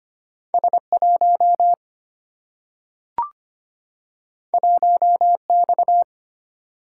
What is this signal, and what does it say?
Morse code, heard as a keyed tone, about 25 wpm, 700 Hz. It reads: S1 1X
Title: Single-Letter Single-Number 25wpm